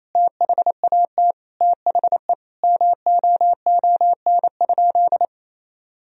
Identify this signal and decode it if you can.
Morse code, heard as a keyed tone, about 27 wpm, 700 Hz. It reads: THAT THE MOON?
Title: Sentences from Top 1000 Words - Review Part 3 - 28wpm